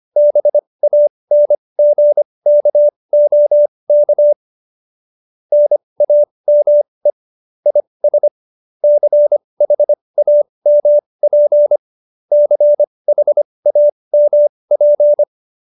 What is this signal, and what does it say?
Morse code, heard as a keyed tone, about 25 wpm, spaced as for 20 wpm, 590 Hz. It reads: BANGKOK NAME IS CHAMP CHAMP
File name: qso208b 20